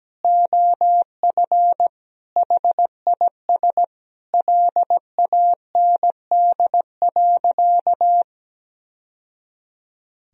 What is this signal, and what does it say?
Morse code, heard as a keyed tone, about 17 wpm, 700 Hz. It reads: OF HIS LAND.
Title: Sentences from Top 1000 Words - Review Part 1 - 17wpm